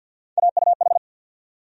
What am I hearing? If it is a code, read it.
Morse code, heard as a keyed tone, about 49 wpm, 700 Hz. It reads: AUH
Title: USA Call Sign Postfixes - 3 Letters 50wpm